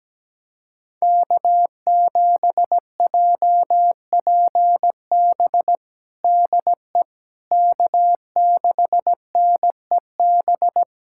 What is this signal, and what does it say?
Morse code, heard as a keyed tone, about 17 wpm, 700 Hz. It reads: K7JPB DE K6NEB
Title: USA Callsign DE Callsign 17wpm